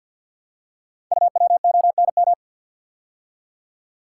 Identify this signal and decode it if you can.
Morse code, heard as a keyed tone, about 48 wpm, 690 Hz. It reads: UWONM
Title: Code Groups - 5 Letters - 50wpm